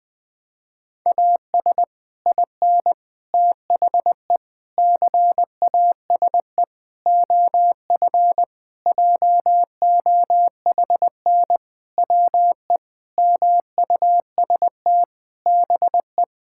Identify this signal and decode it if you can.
Morse code, heard as a keyed tone, about 20 wpm, 700 Hz. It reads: AS IN THE CASE OF JOHN WE MUST BE
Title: Sentences with 5-Word Phrases - 20wpm